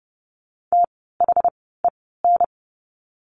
Morse code, heard as a keyed tone, about 30 wpm, spaced as for 17 wpm, 700 Hz. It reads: THEN
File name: course / 30:10wpm / 115 Lesson 19f - Letter w - Words 30wpm